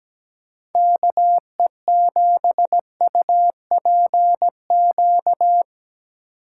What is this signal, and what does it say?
Morse code, heard as a keyed tone, about 18 wpm, 700 Hz. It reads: KE7UPQ
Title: Speed Racing - USA 2x3 Call Signs 17wpm